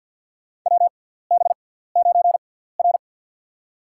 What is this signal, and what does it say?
Morse code, heard as a keyed tone, about 53 wpm, spaced as for 17 wpm, 700 Hz. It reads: WB9R
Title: USA Call Signs - 4 Characters - 6x Character Spacing - 50wpm